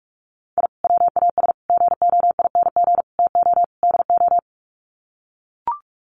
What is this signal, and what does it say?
Morse code, heard as a keyed tone, about 44 wpm, 700 Hz. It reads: I WAS GOING TO DO
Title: Sets of 5 Words Encore2 45wpm